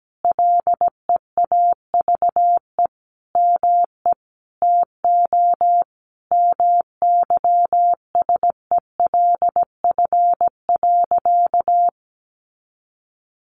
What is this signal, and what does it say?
Morse code, heard as a keyed tone, about 17 wpm, 700 Hz. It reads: LEAVE ME TO MYSELF.
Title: Sentences from Top 1300 Words 17wpm